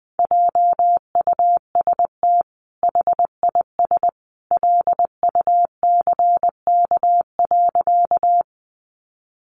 Morse code, heard as a keyed tone, about 19 wpm, 700 Hz. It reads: JUST HIS LUCK.
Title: Sentences from Top 1500 Words 20wpm